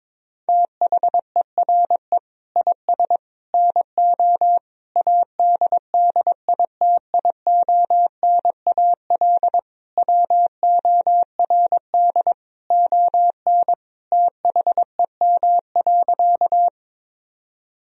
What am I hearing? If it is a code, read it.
Morse code, heard as a keyed tone, about 22 wpm, 700 Hz. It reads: THERE IS NO ADDITIONAL WORD ON THEM.
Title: Sentences from Top 1000 Words - Review Part 4 - 22wpm